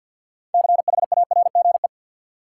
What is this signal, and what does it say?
Morse code, heard as a keyed tone, about 52 wpm, 700 Hz. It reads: CHARGE
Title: Sentences from Top 900 Words - New Word Drill - 50wpm